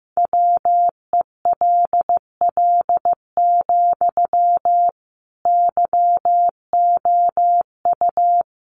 Morse code, heard as a keyed tone, about 15 wpm, 700 Hz. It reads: WELL, YOU
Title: Sentences from Top 1500 Words 15wpm